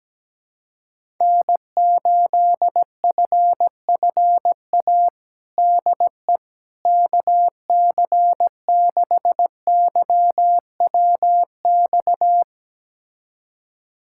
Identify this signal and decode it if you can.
Morse code, heard as a keyed tone, about 17 wpm, 700 Hz. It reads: N8FFA DE KC6YWX
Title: USA Callsign DE Callsign 17wpm